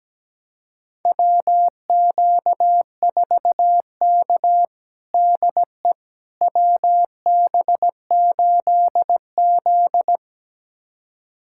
Morse code, heard as a keyed tone, about 17 wpm, 700 Hz. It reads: WQ4K DE WB8Z
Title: USA Callsign DE Callsign - 4 Characters - 17wpm